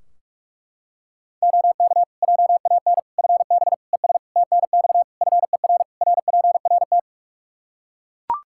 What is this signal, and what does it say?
Morse code, heard as a keyed tone, about 41 wpm, 700 Hz. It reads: OK JAN FB ES TNX FER RPRT
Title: K1USN SST QSO Practice 45wpm